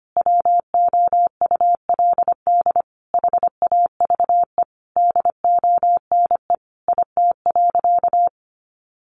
Morse code, heard as a keyed tone, about 25 wpm, 700 Hz. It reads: WOULD HAVE DONE IT.
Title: Sentences from Top 1200 Words 25wpm